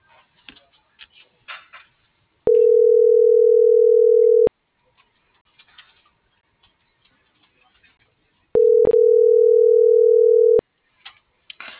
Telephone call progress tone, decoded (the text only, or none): ringback tone